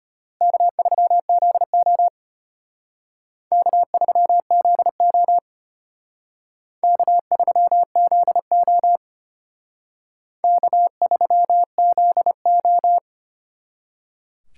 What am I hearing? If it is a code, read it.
Morse code, heard as a keyed tone, about 32 wpm, 700 Hz. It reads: K3ZO K3ZO K3ZO K3ZO